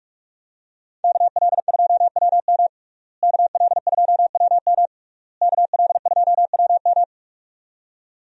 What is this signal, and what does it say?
Morse code, heard as a keyed tone, about 41 wpm, 700 Hz. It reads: KL2WM KL2WM KL2WM
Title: USA Call Signs - 5 Characters - Repeated 3x - 45wpm